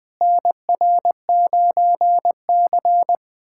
Morse code, heard as a keyed tone, about 20 wpm, 700 Hz. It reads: NR9C
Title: USA Callsign DE Callsign - 4 Characters - 20wpm